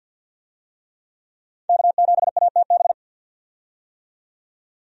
Morse code, heard as a keyed tone, about 51 wpm, 700 Hz. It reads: K7ATB
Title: USA Call Signs - 5 Characters - 50wpm